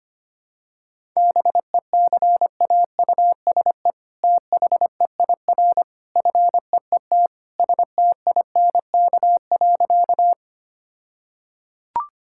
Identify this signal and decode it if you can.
Morse code, heard as a keyed tone, about 25 wpm, 700 Hz. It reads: BECAUSE THEIR FEET STINK.
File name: practice / 25wpm / All-Age Appropriate Jokes - 25wpm